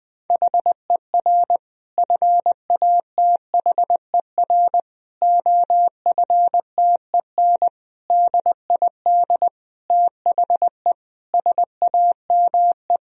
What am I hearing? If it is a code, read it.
Morse code, heard as a keyed tone, about 19 wpm, 700 Hz. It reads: HER FATHER OFTEN DID THE SAME
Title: Sentences from Top 1000 Words - Review Part 1 - 20wpm